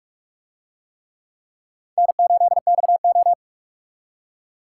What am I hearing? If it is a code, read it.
Morse code, heard as a keyed tone, about 44 wpm, 700 Hz. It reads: N8XO